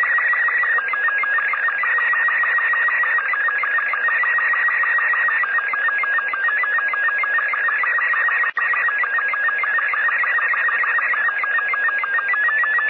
SSTV (slow-scan television), Robot 36.
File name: ARISS SSTV / SuitSat-SSTV